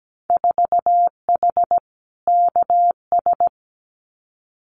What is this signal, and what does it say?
Morse code, heard as a keyed tone, about 17 wpm, 700 Hz. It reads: 4H KS
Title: Winter Field Day Exchanges Encore 17wpm